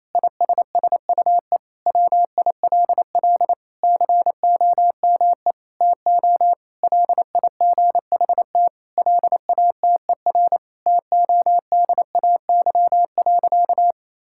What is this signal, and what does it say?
Morse code, heard as a keyed tone, about 27 wpm, 700 Hz. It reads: ISSUE WILL COME TO LIGHT LATER TODAY.
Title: Sentences from Top 1000 Words - Review Part 6 - 28wpm